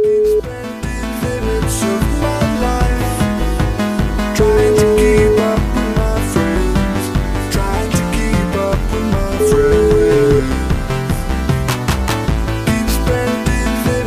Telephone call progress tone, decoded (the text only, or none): ringback tone